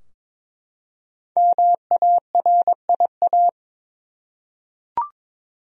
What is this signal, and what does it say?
Morse code, heard as a keyed tone, about 22 wpm, 700 Hz. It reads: MARIA
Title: USA Common Names - Max 5 Letters - 22wpm